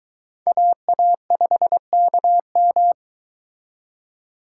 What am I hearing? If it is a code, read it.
Morse code, heard as a keyed tone, about 23 wpm, 700 Hz. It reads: AA5KM